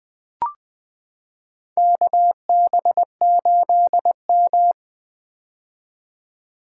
Morse code, heard as a keyed tone, about 20 wpm, 700 Hz. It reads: KB8M
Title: K1USN SST QSO Practice - 20wpm